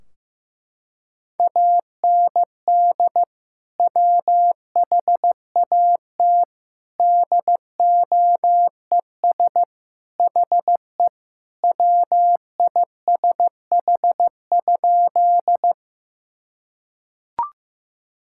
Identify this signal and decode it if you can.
Morse code, heard as a keyed tone, about 15 wpm, 700 Hz. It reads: AND WHAT DOES HE WISH?
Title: Sentences from Top 1600 Words 15wpm